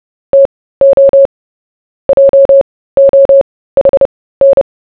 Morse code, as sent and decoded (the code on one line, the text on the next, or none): - --- / .--- --- .... -.
TO JOHN